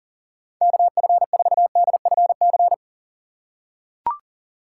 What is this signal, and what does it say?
Morse code, heard as a keyed tone, about 40 wpm, 700 Hz. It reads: KF4DFC
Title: USA Call Signs - 6 Characters - 6x Character Spacing - 40wpm